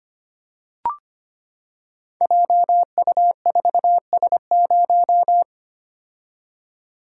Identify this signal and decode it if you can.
Morse code, heard as a keyed tone, about 25 wpm, 700 Hz. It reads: JU4S0
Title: Code Groups - 5 Letter:Numbers - 25wpm